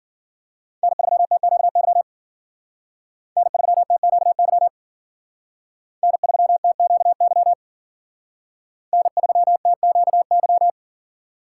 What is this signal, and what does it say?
Morse code, heard as a keyed tone, about 54 wpm, 700 Hz. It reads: N3TQY N3TQY N3TQY N3TQY